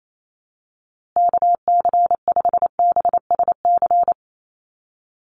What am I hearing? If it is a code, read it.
Morse code, heard as a keyed tone, about 28 wpm, 700 Hz. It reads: KC5BSC
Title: USA Call Signs Repeated 3x 28wpm